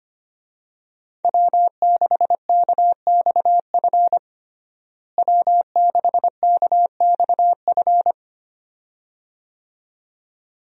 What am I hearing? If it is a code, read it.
Morse code, heard as a keyed tone, about 25 wpm, 700 Hz. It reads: W6KXF W6KXF
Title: USA Call Signs - 5 Characters - Repeated 2x - 25wpm